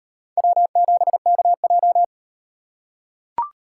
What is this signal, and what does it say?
Morse code, heard as a keyed tone, about 37 wpm, 700 Hz. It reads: W7KJ